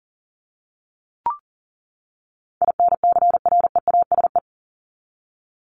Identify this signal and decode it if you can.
Morse code, heard as a keyed tone, about 40 wpm, 700 Hz. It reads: INCREASE
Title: ING Suffix - Top 100 Words with Base Words 40wpm